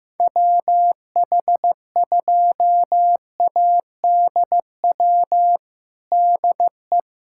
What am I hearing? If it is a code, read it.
Morse code, heard as a keyed tone, about 15 wpm, 700 Hz. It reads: WH2ADW DE
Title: USA Callsign DE Callsign - 6 Characters - 15wpm